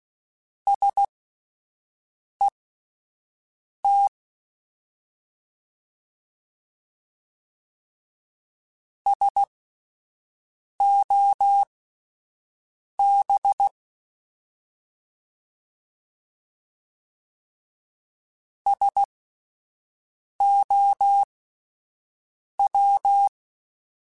Morse code, heard as a keyed tone, about 16 wpm, spaced as for 5 wpm, 800 Hz. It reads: SET SOB SOW